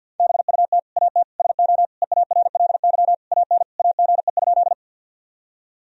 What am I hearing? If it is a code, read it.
Morse code, heard as a keyed tone, about 51 wpm, 700 Hz. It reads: BUT AT SO EARLY AN AGE?
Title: Sentences from Top 1000 Words - Review Part 3 - 50wpm